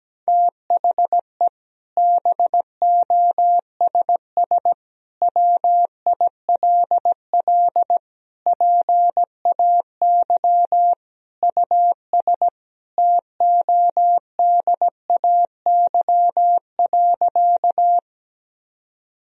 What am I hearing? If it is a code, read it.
Morse code, heard as a keyed tone, about 17 wpm, 700 Hz. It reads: THE BOSS WILL PAY US TODAY.
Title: Sentences from Top 1000 Words - Review Part 4 - 17wpm